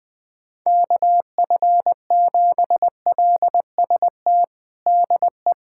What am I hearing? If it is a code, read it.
Morse code, heard as a keyed tone, about 20 wpm, 700 Hz. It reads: KF7LST DE
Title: USA Callsign DE Callsign 20wpm